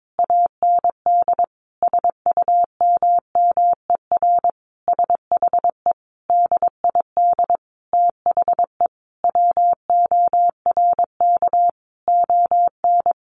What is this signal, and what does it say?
Morse code, heard as a keyed tone, about 22 wpm, 700 Hz. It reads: AND SUMMER SHE DID THE WORK ON
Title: Sentences from Top 1200 Words 22wpm